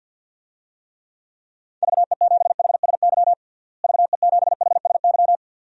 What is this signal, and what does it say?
Morse code, heard as a keyed tone, about 49 wpm, 700 Hz. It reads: VE7HSY VE7HSY
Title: DXCC-100 Call Signs - 6 Characters - Repeated 2x - 50wpm